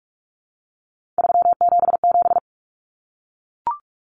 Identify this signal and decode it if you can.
Morse code, heard as a keyed tone, about 44 wpm, 700 Hz. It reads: 377
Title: Numbers - 3-Digit 45wpm